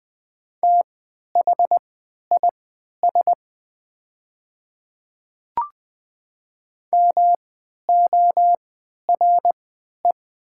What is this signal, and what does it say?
Morse code, heard as a keyed tone, about 20 wpm, spaced as for 11 wpm, 700 Hz. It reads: THIS MORE
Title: 3x Character Spacing - Top 100 Words - 20wpm